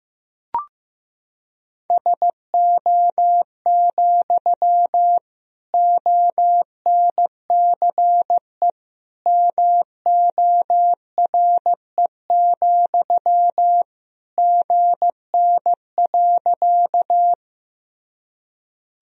Morse code, heard as a keyed tone, about 15 wpm, 700 Hz. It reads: SO, ONCE MORE, GN.
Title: Sentences from Top 300 Words and Abbreviations 15wpm